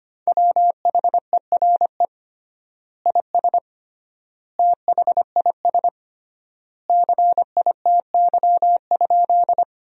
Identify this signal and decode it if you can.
Morse code, heard as a keyed tone, about 25 wpm, 700 Hz. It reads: WHERE IS THIS CITY?
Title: 3x Word-Spacing - Sentences from Top 300 Words - 25wpm